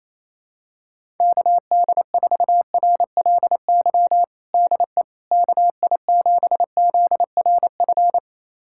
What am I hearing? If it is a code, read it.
Morse code, heard as a keyed tone, about 28 wpm, 700 Hz. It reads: KD4RLY DE KI7ZRF
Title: USA Callsign DE Callsign 28wpm